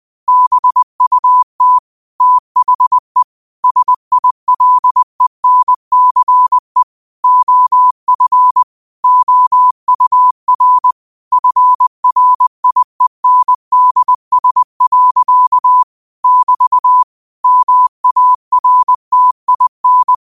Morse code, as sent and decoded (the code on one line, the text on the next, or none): -... ..- - / - .... / ... .. .-.. -. -.-. / --- ..-. / --- ..- .-. / ..-. .-. .. -. -.. ... .-.-.- / -...- / -- .- .-. - .. -.
BUT THE SILENCE OF OUR FRIENDS. <BT> MARTIN